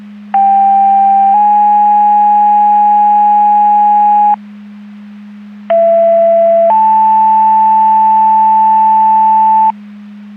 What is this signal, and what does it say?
Two-tone page, heard as a one-second tone, then a three-second tone, about 790 then 830 Hz, about 670 then 880 Hz. It is heard from 0.3 to 9.7 s.
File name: Portage / 01464 17-46 Portage 154.385 Paging